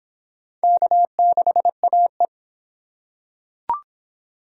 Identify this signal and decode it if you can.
Morse code, heard as a keyed tone, about 26 wpm, 700 Hz. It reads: K6AE